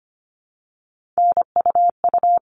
Morse code, heard as a keyed tone, about 25 wpm, 700 Hz. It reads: NUU
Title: All Cut Numbers - 3-Digit 25wpm